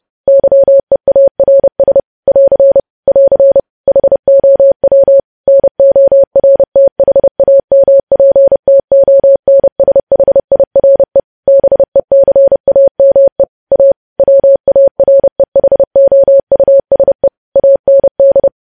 Morse code, heard as a keyed tone, about 30 wpm, 550 Hz. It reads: YEARS <AR> <AR> HOW NORTHAMPTONSHIRE BECAME A WAREHOUSE AND
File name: podcast / News headlines-1x-30x30